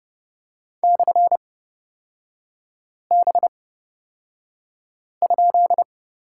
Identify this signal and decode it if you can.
Morse code, heard as a keyed tone, about 30 wpm, spaced as for 5 wpm, 700 Hz. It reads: /B?
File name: Lesson 38e - Letter q - Three Characters 30wpm